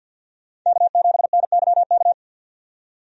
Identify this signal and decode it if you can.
Morse code, heard as a keyed tone, about 53 wpm, 700 Hz. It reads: K7NYK
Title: USA Call Signs - 5 Characters - 50wpm